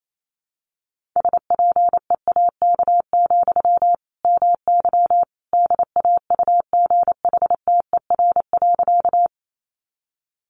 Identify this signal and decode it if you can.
Morse code, heard as a keyed tone, about 28 wpm, 700 Hz. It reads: SPEAK, MY DAUGHTER.
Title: Sentences from Top 1500 Words 28wpm